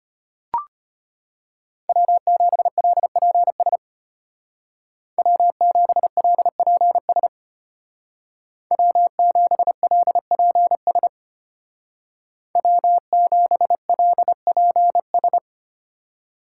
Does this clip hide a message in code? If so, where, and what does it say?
Yes: Morse code from 1.9 to 15.4 s, heard as a keyed tone, about 33 wpm, 700 Hz. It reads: W7LPS W7LPS W7LPS W7LPS